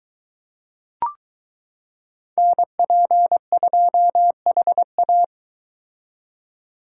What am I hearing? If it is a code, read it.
Morse code, heard as a keyed tone, about 23 wpm, 700 Hz. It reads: NP2HA